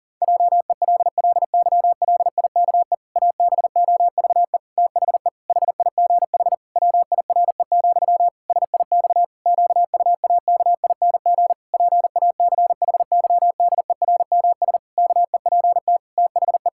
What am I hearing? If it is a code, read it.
Morse code, heard as a keyed tone, about 40 wpm, 700 Hz. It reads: JELLYLIKE ABOVE THE HIGH WIRE, SIX QUAKING PACHYDERMS KEPT THE
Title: Pangram Challenge v2 - 40wpm